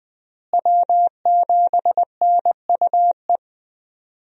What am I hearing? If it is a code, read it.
Morse code, heard as a keyed tone, about 20 wpm, 700 Hz. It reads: W7NUE
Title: USA Call Signs - 5 Characters - Repeated 2x - 20wpm